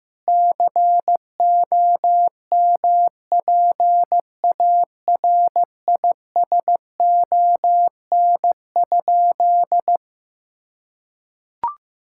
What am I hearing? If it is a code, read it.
Morse code, heard as a keyed tone, about 16 wpm, 700 Hz. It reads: COMPARISON?
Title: Sentences from Top 1600 Words 15wpm